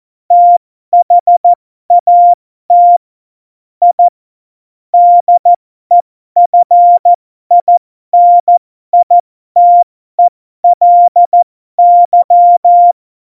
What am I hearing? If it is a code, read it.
Morse code, heard as a keyed tone, about 14 wpm, spaced as for 12 wpm, 700 Hz. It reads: THAT I DEFINITELY
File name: Morsekurs / Witze